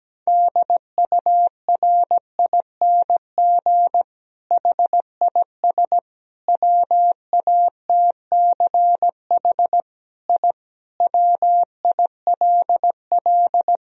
Morse code, heard as a keyed tone, about 17 wpm, 700 Hz. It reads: DURING HIS WATCH I WILL
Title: Sentences from Top 1000 Words - Review Part 2 - 17wpm